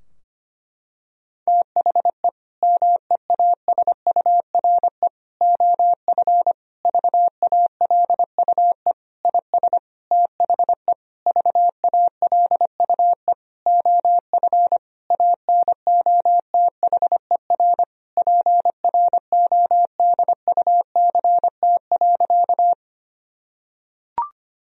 Morse code, heard as a keyed tone, about 25 wpm, 700 Hz. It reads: THE MEASURE OF VALUE IS THE VALUE OF ANOTHER PRODUCT.
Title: Sentences from Top 1000 Words - Review Part 3 - 25wpm